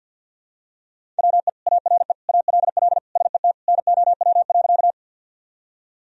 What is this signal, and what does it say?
Morse code, heard as a keyed tone, about 46 wpm, 700 Hz. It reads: WE ARE ALL SET NOW.
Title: Sentences from Top 1000 Words - Review Part 1 - 50wpm